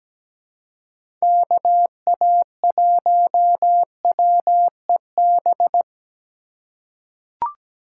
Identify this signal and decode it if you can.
Morse code, heard as a keyed tone, about 17 wpm, 700 Hz. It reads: KA1WEB
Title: USA Call Signs - 6 Characters - 3x Character Spacing - 17wpm